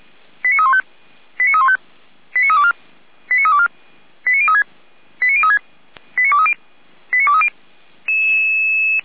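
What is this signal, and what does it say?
Five-tone selective call (ZVEI), heard as quick runs of five tones, tones of about 70 ms. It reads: 89315 89315 89324 89324 89036 89036 89230 89230
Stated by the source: ZVEI five-tone selective call received on an SDR.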